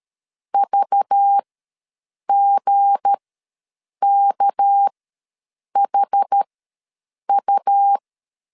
Morse code, heard as a keyed tone, about 13 wpm, spaced as for 7 wpm, 800 Hz. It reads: VGKHU
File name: LEZIONI / 8 YJ45 / 8 YJ ws 12wpm 3342